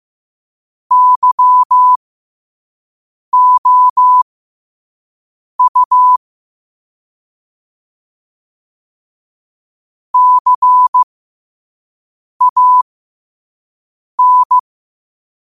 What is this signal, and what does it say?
Morse code, heard as a keyed tone, about 15 wpm, spaced as for 5 wpm, 1000 Hz. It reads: YOU CAN
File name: qotdfiles / 5wpm